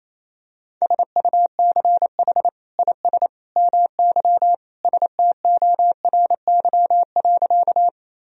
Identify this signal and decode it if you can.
Morse code, heard as a keyed tone, about 28 wpm, 700 Hz. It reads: SUCH IS MY STORY.
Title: Sentences from Top 1000 Words - Review Part 1 - 28wpm